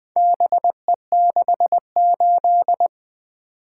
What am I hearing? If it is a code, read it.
Morse code, heard as a keyed tone, about 20 wpm, 700 Hz. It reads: BE68
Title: Four Letter-Number Combinations - 20wpm